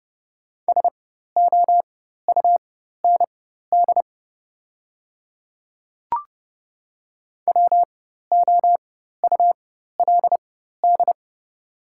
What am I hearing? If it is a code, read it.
Morse code, heard as a keyed tone, about 30 wpm, spaced as for 14 wpm, 700 Hz. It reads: SOUND WOULD
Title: 4x Character Spacing - Top 100 Words - 30wpm